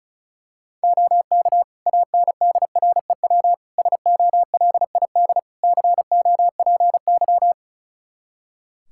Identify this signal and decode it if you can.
Morse code, heard as a keyed tone, about 36 wpm, 700 Hz. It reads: OK ANDREW SOLID COPY
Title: Practice QSO - Exchange 3 - More Info - 35wpm